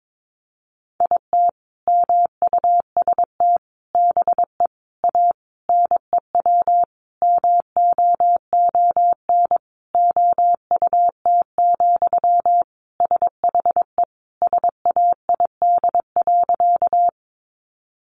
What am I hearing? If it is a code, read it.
Morse code, heard as a keyed tone, about 22 wpm, 700 Hz. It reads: IT MUST BE A NEW MOON OUT, SHE SAID.
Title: Sentences from Top 500 Words 22wpm